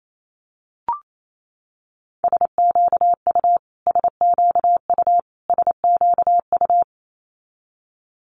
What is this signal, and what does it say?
Morse code, heard as a keyed tone, about 28 wpm, 700 Hz. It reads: SQU SQU SQU
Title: Multi-letter Phonemes Repeated 3x 28wpm